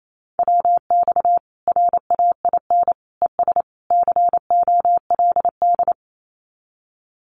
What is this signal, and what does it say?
Morse code, heard as a keyed tone, about 28 wpm, 700 Hz. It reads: WX RAIN ES COLD
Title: Common QSO Phrases 28wpm